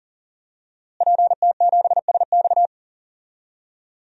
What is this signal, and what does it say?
Morse code, heard as a keyed tone, about 40 wpm, 700 Hz. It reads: PT7SX
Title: DXCC-100 Call Signs - 5 Characters - Repeated 2x - 40wpm